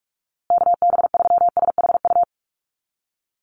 Morse code, heard as a keyed tone, about 44 wpm, 700 Hz. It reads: KB3SHU
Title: USA Call Signs Repeated 3x 45wpm